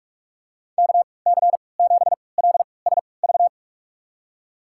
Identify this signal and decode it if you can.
Morse code, heard as a keyed tone, about 41 wpm, spaced as for 25 wpm, 690 Hz. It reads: KC7LSV
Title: USA Call Signs - 6 Characters - 3x Character Spacing - 45wpm